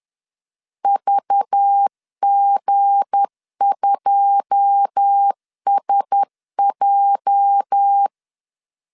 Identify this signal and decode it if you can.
Morse code, heard as a keyed tone, about 10 wpm, 800 Hz. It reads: VG2SJ
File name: CW / DET / 1 9 1500C S C 10wpm 4557